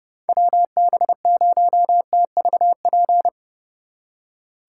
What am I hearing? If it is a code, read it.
Morse code, heard as a keyed tone, about 40 wpm, 700 Hz. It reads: WB0TVP